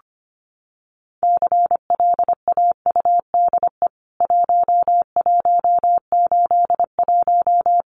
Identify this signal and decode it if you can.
Morse code, heard as a keyed tone, about 25 wpm, 700 Hz. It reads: CLAUDE 1181
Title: CWT Exchange 25wpm